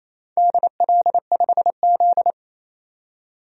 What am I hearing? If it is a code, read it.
Morse code, heard as a keyed tone, about 28 wpm, 700 Hz. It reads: DL5Z
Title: DXCC-100 Call Signs - 4 Characters - 28wpm